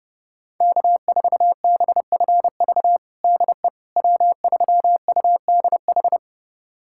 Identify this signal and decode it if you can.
Morse code, heard as a keyed tone, about 32 wpm, 700 Hz. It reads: K4BFV DE W3UDH
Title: USA Callsign DE Callsign - 5 Characters - 30wpm